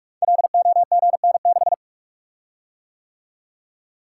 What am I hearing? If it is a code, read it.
Morse code, heard as a keyed tone, about 44 wpm, 700 Hz. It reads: LOGN6
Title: Code Groups - 5 Letter:Numbers - 45wpm